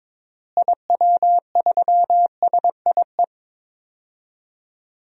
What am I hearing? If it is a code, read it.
Morse code, heard as a keyed tone, about 21 wpm, 700 Hz. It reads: IW3SIE
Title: DXCC-100 Call Signs - 6 Characters - Repeated 3x - 22wpm